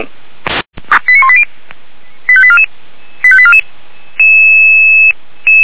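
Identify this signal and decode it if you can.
Five-tone selective call (ZVEI), heard as quick runs of five tones, tones of about 70 ms. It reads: 89180 85844 85844
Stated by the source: ZVEI five-tone selective call received on SDR, from feuerwehr-orlamuende.org (fire brigade Orlamünde).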